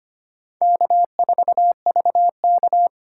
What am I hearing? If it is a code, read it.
Morse code, heard as a keyed tone, about 25 wpm, 700 Hz. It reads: K4VK
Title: USA Call Signs - 4x Character Spacing - 25wpm